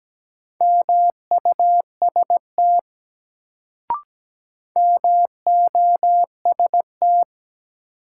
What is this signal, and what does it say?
Morse code, heard as a keyed tone, about 17 wpm, 700 Hz. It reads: MUST MOST
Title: Sets of 2 Words - New Word Drill - 17wpm